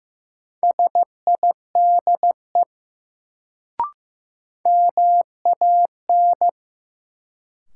Morse code, heard as a keyed tone, about 15 wpm, 700 Hz. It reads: SIDE MAN
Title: Sets of 2 Words - New Word Drill - 15wpm